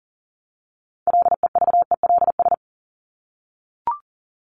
Morse code, heard as a keyed tone, about 40 wpm, 700 Hz. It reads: LEVELS
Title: ES:S Suffix - Top 100 Words with Base Words 40wpm